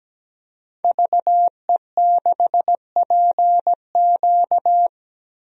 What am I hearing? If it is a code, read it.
Morse code, heard as a keyed tone, about 17 wpm, 700 Hz. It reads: VE6PQ
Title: DXCC-100 Call Signs - 5 Characters - Repeated 3x - 17wpm